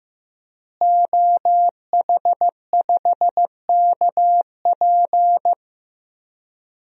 Morse code, heard as a keyed tone, about 15 wpm, 700 Hz. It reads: OH5KP
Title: DXCC-100 Call Signs - 5 Characters - Repeated 2x - 15wpm